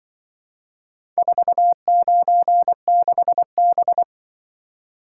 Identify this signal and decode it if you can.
Morse code, heard as a keyed tone, about 24 wpm, 700 Hz. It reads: 496B
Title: Speed-Racing - Four Letter-Number Combinations - 20wpm